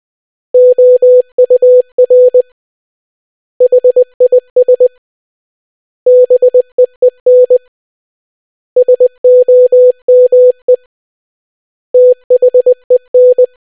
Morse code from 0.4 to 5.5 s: OUR HIS